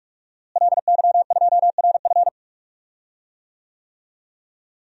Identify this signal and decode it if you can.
Morse code, heard as a keyed tone, about 44 wpm, 700 Hz. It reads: LY2RF
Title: DXCC-100 Call Signs - 5 Characters - Repeated 2x - 45wpm